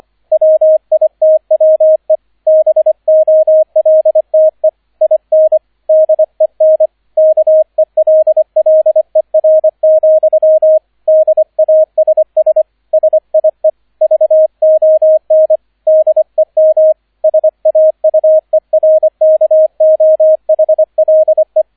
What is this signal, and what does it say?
Morse code, heard as a keyed tone, about 25 wpm, 630 Hz, with radio noise. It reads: WITWE BOLTE IN DEN KELLER, DASS SIE VON DEM SAUERKOHLE